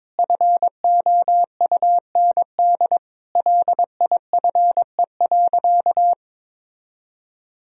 Morse code, heard as a keyed tone, about 22 wpm, 700 Hz. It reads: FOUND LIFE.
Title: Sentences from Top 1000 Words - Review Part 2 - 22wpm